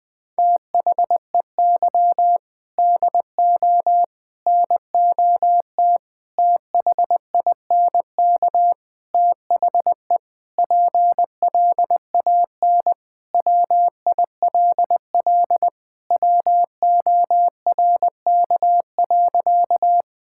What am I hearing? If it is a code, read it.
Morse code, heard as a keyed tone, about 20 wpm, 700 Hz. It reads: THEY DO NOT THINK THE PLAN WILL WORK.
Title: Sentences with 4-Word Phrases - Spoken Beginning - 20wpm